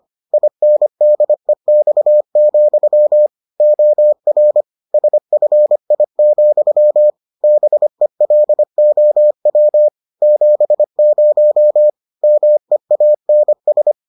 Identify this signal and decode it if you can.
Morse code, heard as a keyed tone, about 25 wpm, 600 Hz. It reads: INDEX, OR SFI, BELOW 70 MEANS